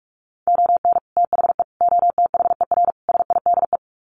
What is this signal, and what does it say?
Morse code, heard as a keyed tone, about 44 wpm, 700 Hz. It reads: ON THE OTHER SIDE
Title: Sets of 4 Words 45wpm